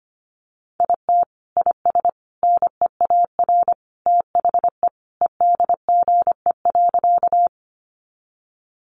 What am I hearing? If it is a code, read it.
Morse code, heard as a keyed tone, about 25 wpm, 700 Hz. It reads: IT IS NEAR THE EDGE.
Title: Additional Easy Phrases 25wpm